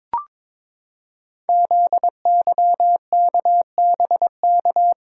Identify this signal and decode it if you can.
Morse code, heard as a keyed tone, about 23 wpm, 700 Hz. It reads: ZYKBK
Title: Code Groups - 5 Letters - 22wpm